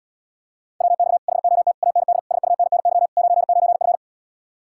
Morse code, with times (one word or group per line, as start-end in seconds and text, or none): 0.8-1.2 WX
1.3-1.7 HOT
1.8-2.2 AND
2.3-3.1 SUNNY
3.2-4.0 99F